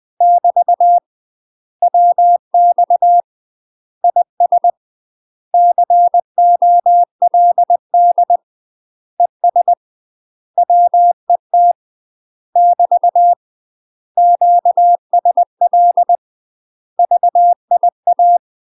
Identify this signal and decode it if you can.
Morse code, heard as a keyed tone, about 20 wpm, 700 Hz. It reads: <BT> WX IS COLD ES WET <BT> QSL VIA
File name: UK MASTER QSOs TRAINING / 20WPM / UK MASTER QSOs 0055